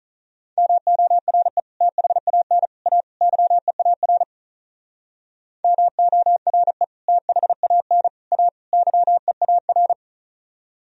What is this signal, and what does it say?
Morse code, heard as a keyed tone, about 46 wpm, 700 Hz. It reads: MORE THAN A YEAR MORE THAN A YEAR